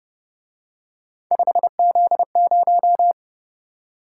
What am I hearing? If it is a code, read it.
Morse code, heard as a keyed tone, about 28 wpm, 700 Hz. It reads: 5Z0